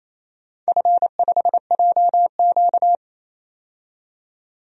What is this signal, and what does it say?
Morse code, heard as a keyed tone, about 28 wpm, 700 Hz. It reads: F5JQ
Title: DXCC-100 Call Signs - 4 Characters - 28wpm